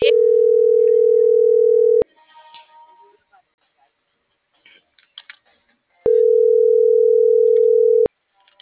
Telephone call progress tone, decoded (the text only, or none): ringback tone